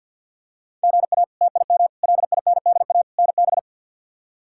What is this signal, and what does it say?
Morse code, heard as a keyed tone, about 46 wpm, 700 Hz. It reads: GA TIM LINDA NB
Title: K1USN SST QSO Practice - 50wpm